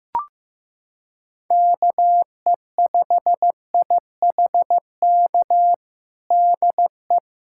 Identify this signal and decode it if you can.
Morse code, heard as a keyed tone, about 15 wpm, 700 Hz. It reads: KE5IHK DE
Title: USA Callsign DE Callsign - 6 Characters - 15wpm